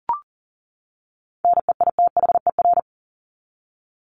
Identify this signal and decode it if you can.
Morse code, heard as a keyed tone, about 40 wpm, 700 Hz. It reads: NEITHER
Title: Top 1501-1750 Words 40wpm